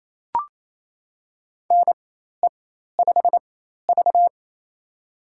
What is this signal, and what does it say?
Morse code, heard as a keyed tone, about 28 wpm, spaced as for 13 wpm, 700 Hz. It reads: NE5V
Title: USA Call Signs - 4x Character Spacing - 28wpm